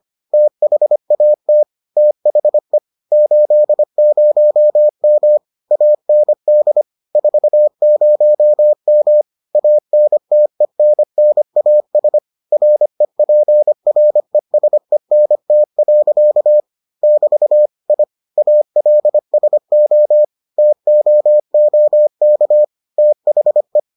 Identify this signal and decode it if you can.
Morse code, heard as a keyed tone, about 25 wpm, 600 Hz. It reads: THAT THE 80M AND 40M ANTENNAS REPRESENT. <BT> I ALSO TOOK THE